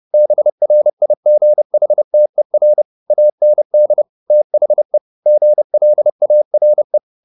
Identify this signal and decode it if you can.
Morse code, heard as a keyed tone, about 30 wpm, 600 Hz. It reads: BRIGHTER AND THE GLARE
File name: the old man and the sea / the old man and the sea 1 30wpm